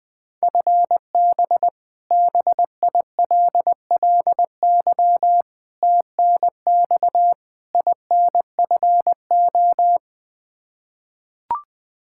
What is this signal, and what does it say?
Morse code, heard as a keyed tone, about 20 wpm, 700 Hz. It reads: FB BILLY TNX INFO
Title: K1USN SST QSO Practice 20wpm